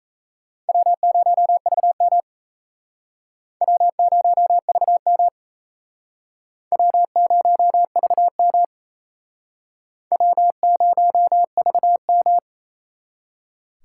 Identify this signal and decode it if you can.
Morse code, heard as a keyed tone, about 37 wpm, 700 Hz. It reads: W0VM W0VM W0VM W0VM